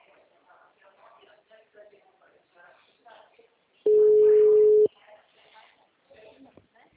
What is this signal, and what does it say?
Ringback tone (Europe, 425 Hz), with speech.
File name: out- -2003- - - .51280